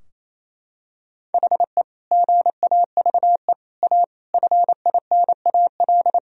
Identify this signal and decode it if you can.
Morse code, heard as a keyed tone, about 28 wpm, 700 Hz. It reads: HE GAVE A FINAL
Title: Sentences from Top 1000 Words - Review Part 2 - 28wpm